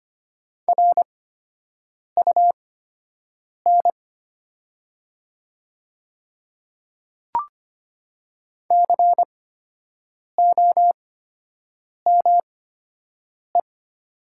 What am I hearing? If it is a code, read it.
Morse code, heard as a keyed tone, about 25 wpm, spaced as for 7 wpm, 700 Hz. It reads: RUN COME
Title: 8x Character Spacing - Top 300 Words - 25wpm